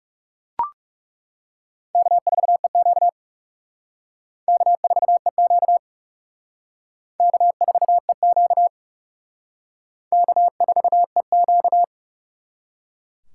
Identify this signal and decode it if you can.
Morse code, heard as a keyed tone, about 38 wpm, 700 Hz. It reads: K4EQ K4EQ K4EQ K4EQ